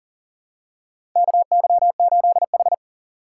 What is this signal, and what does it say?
Morse code, heard as a keyed tone, about 40 wpm, 700 Hz. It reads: KY8H
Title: USA Call Signs - 4x Character Spacing - 40wpm